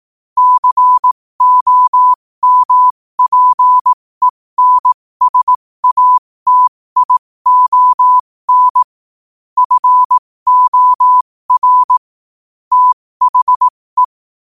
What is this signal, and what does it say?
Morse code, heard as a keyed tone, about 18 wpm, spaced as for 15 wpm, 1000 Hz. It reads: COMPENSATION FOR THE
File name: qotdfiles / 15wpm